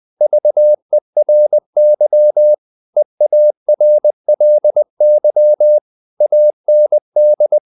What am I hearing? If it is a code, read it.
Morse code, heard as a keyed tone, about 20 wpm, 600 Hz. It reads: VERY EARLY AND
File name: the old man and the sea / the old man and the sea part 1 20wpm